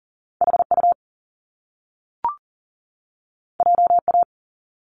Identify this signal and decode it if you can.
Morse code, heard as a keyed tone, about 42 wpm, 700 Hz. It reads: HU JA
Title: Top Two Letter Combinations in Words 40wpm